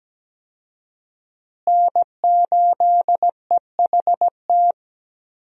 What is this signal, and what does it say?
Morse code, heard as a keyed tone, about 17 wpm, 700 Hz. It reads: N8EHT
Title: USA Call Signs - 5 Characters - Repeated 3x - 17wpm